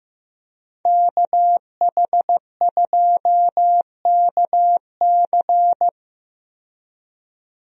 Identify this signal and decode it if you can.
Morse code, heard as a keyed tone, about 15 wpm, 700 Hz. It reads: KH2KC
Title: USA Call Signs - 5 Characters - 15wpm